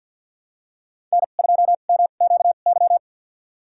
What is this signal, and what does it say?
Morse code, heard as a keyed tone, about 51 wpm, spaced as for 36 wpm, 700 Hz. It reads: N2MQY
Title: USA Call Signs - 5 Characters - 2x Character Spacing - 50wpm